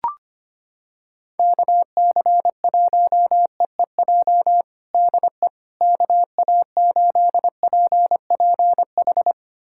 Morse code, heard as a keyed tone, about 25 wpm, 700 Hz. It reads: KC1EEJ DE KA8PPH
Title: USA Callsign DE Callsign - 6 Characters - 25wpm